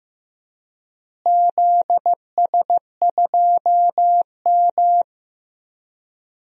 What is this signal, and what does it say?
Morse code, heard as a keyed tone, about 16 wpm, 700 Hz. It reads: ZS2M
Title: DXCC-100 Call Signs - 4 Characters - Repeated 3x - 15wpm